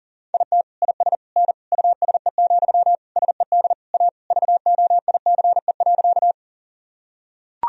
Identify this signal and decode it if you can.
Morse code, heard as a keyed tone, about 37 wpm, 700 Hz. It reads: IT IS N USE, SED A VOICE.
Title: Sentences from Top 500 Words and Abbreviations 40wpm